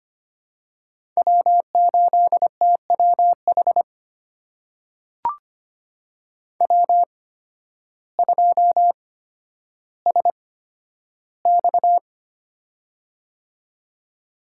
Morse code, heard as a keyed tone, about 25 wpm, spaced as for 12 wpm, 700 Hz. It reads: W8TWH W2SX
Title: USA Call Signs - 8x Character Spacing - 25wpm